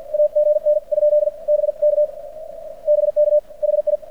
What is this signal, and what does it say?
Morse code, heard as a keyed tone, about 48 wpm, 600 Hz. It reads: A GN ? B K BK HI
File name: archived / CRSnet